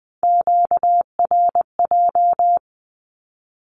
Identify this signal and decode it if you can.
Morse code, heard as a keyed tone, about 20 wpm, 700 Hz. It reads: QRJ
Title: Q-Codes Repeated 3x 20wpm